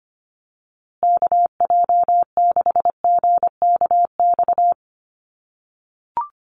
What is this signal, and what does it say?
Morse code, heard as a keyed tone, about 25 wpm, 700 Hz. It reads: KJ6GKX
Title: USA Call Signs Repeated 3x 25wpm